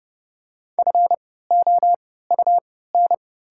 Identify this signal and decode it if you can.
Morse code, heard as a keyed tone, about 32 wpm, spaced as for 17 wpm, 690 Hz. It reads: FOUN
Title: 3x Character Spacing - Top 500 Words - 30wpm